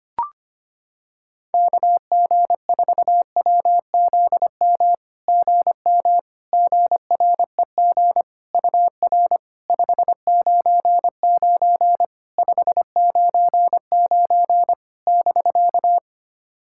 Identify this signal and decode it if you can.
Morse code, heard as a keyed tone, about 25 wpm, 700 Hz. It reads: KG4WZM GM GREG UR 599 599 <BK>
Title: SOTA QSO Practice 25wpm